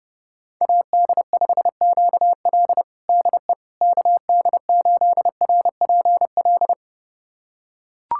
Morse code, heard as a keyed tone, about 32 wpm, 700 Hz. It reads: AD5QL DE KD8RPL
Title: USA Callsign DE Callsign 30wpm